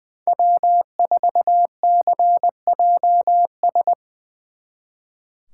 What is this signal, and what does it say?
Morse code, heard as a keyed tone, about 23 wpm, 700 Hz. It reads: W4CJS
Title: Speed Racing - USA 1x3 Call Signs 20wpm